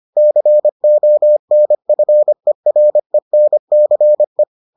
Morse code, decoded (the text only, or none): CONFERENCE